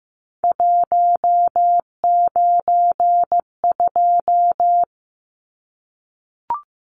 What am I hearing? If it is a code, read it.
Morse code, heard as a keyed tone, about 15 wpm, 700 Hz. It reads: 192
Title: Numbers - 3-Digit 15wpm